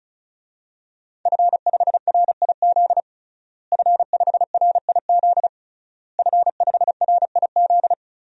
Morse code, heard as a keyed tone, about 36 wpm, 700 Hz. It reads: F5RIZ F5RIZ F5RIZ
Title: DXCC-100 Call Signs - 5 Characters - Repeated 3x - 35wpm